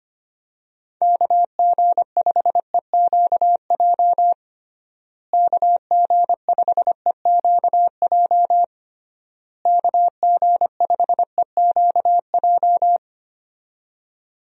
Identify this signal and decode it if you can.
Morse code, heard as a keyed tone, about 25 wpm, 700 Hz. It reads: KG5EQJ KG5EQJ KG5EQJ
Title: USA Call Signs - 6 Characters - Repeated 3x - 25wpm